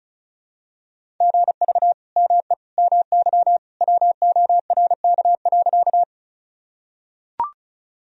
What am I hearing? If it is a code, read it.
Morse code, heard as a keyed tone, about 34 wpm, 700 Hz. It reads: GV ME MY WORK.
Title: Sentences from Top 200 Words and Abbreviations 35wpm